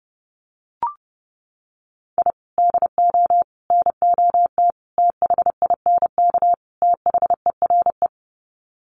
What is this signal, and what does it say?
Morse code, heard as a keyed tone, about 30 wpm, 700 Hz. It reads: I DO NOT THINK THERE
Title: Sets of 5 Words 30wpm